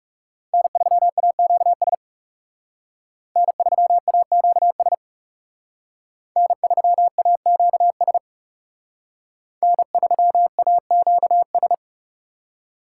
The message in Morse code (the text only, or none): N3AQS N3AQS N3AQS N3AQS